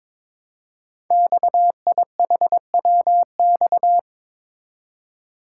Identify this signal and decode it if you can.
Morse code, heard as a keyed tone, about 22 wpm, 700 Hz. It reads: XIHWX
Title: Code Groups - 5 Letters - 22wpm